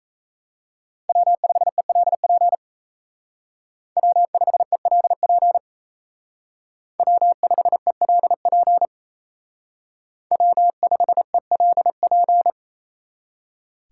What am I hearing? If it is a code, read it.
Morse code, heard as a keyed tone, about 34 wpm, 700 Hz. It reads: W5ELP W5ELP W5ELP W5ELP
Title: Speed Racing - USA 1x3 Call Signs 28wpm